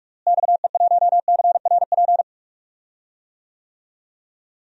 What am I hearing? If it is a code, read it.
Morse code, heard as a keyed tone, about 41 wpm, 700 Hz. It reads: XE1CRP
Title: DXCC-100 Call Signs - 6 Characters - Repeated 3x - 45wpm